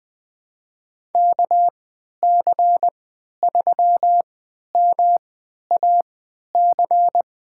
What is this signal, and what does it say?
Morse code, heard as a keyed tone, about 21 wpm, spaced as for 11 wpm, 700 Hz. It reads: KC3MAC
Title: USA Call Signs - 6 Characters - 3x Character Spacing - 20wpm